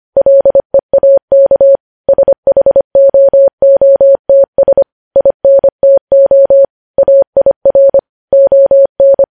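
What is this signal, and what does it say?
Morse code, heard as a keyed tone, about 25 wpm, 550 Hz. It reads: LEAK SHOOTS INTO AIR ON
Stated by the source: Morse code news headlines sent at 25, opening with vvv.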